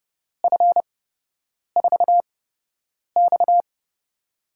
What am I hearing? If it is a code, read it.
Morse code, heard as a keyed tone, about 30 wpm, spaced as for 8 wpm, 700 Hz. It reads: F4X
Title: USA Call Signs - 4 Characters - 8x Character Spacing - 30wpm